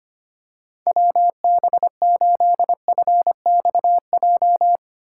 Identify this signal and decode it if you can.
Morse code, heard as a keyed tone, about 25 wpm, 700 Hz. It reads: WB8FXJ
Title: K1USN SST QSO Practice - 25wpm